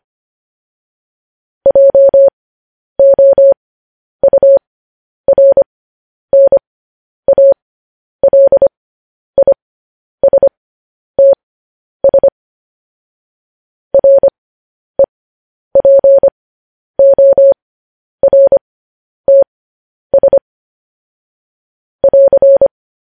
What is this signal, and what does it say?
Morse code, heard as a keyed tone, about 25 wpm, spaced as for 10 wpm, 550 Hz. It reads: JOURNALISTS REPORTS <AR>